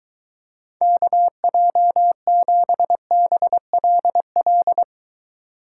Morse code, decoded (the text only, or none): KJ7BLL